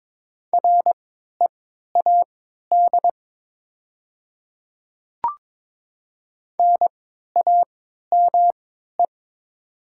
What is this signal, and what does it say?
Morse code, heard as a keyed tone, about 21 wpm, spaced as for 12 wpm, 700 Hz. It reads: READ NAME
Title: 3x Character Spacing - Top 200 Words - 22wpm